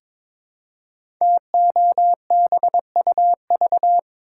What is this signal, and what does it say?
Morse code, heard as a keyed tone, about 22 wpm, 700 Hz. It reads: TOBUV
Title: Code Groups - 5 Letters - 22wpm